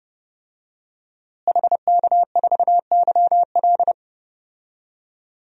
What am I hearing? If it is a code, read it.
Morse code, heard as a keyed tone, about 30 wpm, 700 Hz. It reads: HK4YL
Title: DXCC-100 Call Signs - 5 Characters - Repeated 3x - 30wpm